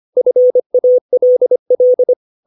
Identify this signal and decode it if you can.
Morse code, heard as a keyed tone, about 25 wpm, 500 Hz. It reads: FALL
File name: 632 words / 632 Words 25WPM